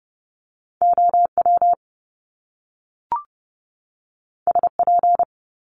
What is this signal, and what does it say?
Morse code, heard as a keyed tone, about 30 wpm, 700 Hz. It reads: OW SP
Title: Top Two Letter Combinations in Words 30wpm